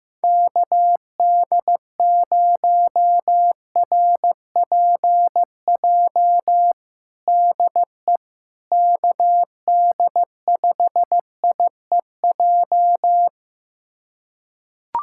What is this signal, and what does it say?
Morse code, heard as a keyed tone, about 15 wpm, 700 Hz. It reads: KD0RPJ DE KD5IEJ
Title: USA Callsign DE Callsign - 6 Characters - 15wpm